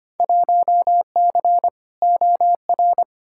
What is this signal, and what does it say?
Morse code, heard as a keyed tone, about 25 wpm, 700 Hz. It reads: 1C OR
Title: Field Day Exchanges - 25wpm